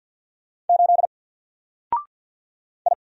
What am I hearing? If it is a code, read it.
Morse code, heard as a keyed tone, about 50 wpm, 700 Hz. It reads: 8 I
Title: Single Character (Letter:Number:Punctuation:Prosign) 50wpm